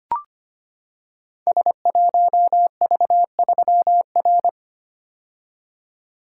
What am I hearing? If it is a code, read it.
Morse code, heard as a keyed tone, about 25 wpm, 700 Hz. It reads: S1V3R
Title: Code Groups - 5 Letter:Numbers - 25wpm